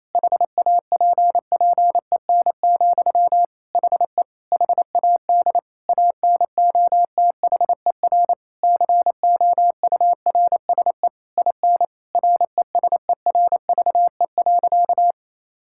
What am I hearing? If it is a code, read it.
Morse code, heard as a keyed tone, about 28 wpm, 700 Hz. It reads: HAPPEN, HE HAD ANOTHER COURSE IN RESERVE.
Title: Sentences from Top 1000 Words - Review Part 6 - 28wpm